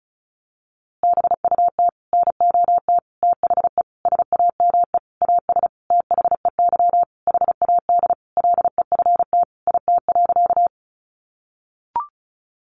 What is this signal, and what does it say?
Morse code, heard as a keyed tone, about 34 wpm, 700 Hz. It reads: BUT NOT THE SAME AS THEY HAD LEFT IT.
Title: A Sound Of Thunder 35wpm